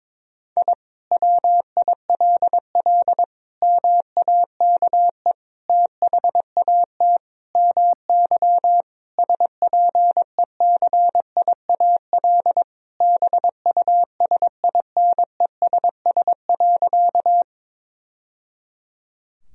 Morse code, heard as a keyed tone, about 22 wpm, 700 Hz. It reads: I WILL MAKE THAT MY SPECIAL BUSINESS.
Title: Sentences from Top 1000 Words - Review Part 3 - 22wpm